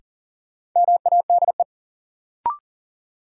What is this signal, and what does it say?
Morse code, heard as a keyed tone, about 39 wpm, 700 Hz. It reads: MADE
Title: Sentences from Top 200 Words - New Word Drill - 40wpm